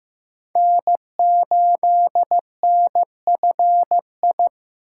Morse code, heard as a keyed tone, about 15 wpm, 700 Hz. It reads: N8NFI
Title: USA Call Signs - 8x Character Spacing - 15wpm